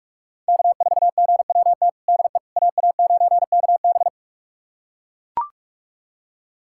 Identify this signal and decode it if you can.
Morse code, heard as a keyed tone, about 43 wpm, 700 Hz. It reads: K4GWT DE AA9KB
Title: USA Callsign DE Callsign 45wpm